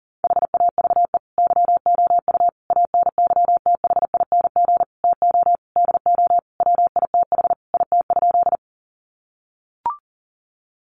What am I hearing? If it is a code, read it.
Morse code, heard as a keyed tone, about 40 wpm, 700 Hz. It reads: HAVE YOU ANYTHING TO DO WITH IT?
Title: Sentences from Top 1000 Words 40wpm